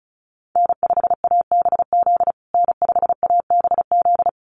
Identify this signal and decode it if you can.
Morse code, heard as a keyed tone, about 34 wpm, 700 Hz. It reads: N5ABZ N5ABZ
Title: USA Call Signs Repeated 2x 35wpm